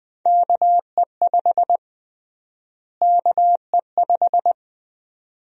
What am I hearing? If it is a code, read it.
Morse code, heard as a keyed tone, about 20 wpm, 700 Hz. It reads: KE5 KE5
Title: USA Call Sign Prefixes - Repeated 2x - 20wpm